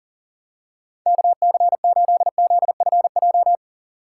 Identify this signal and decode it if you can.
Morse code, heard as a keyed tone, about 40 wpm, 700 Hz. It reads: KC8ZFJ
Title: USA Call Signs - 4x Character Spacing - 40wpm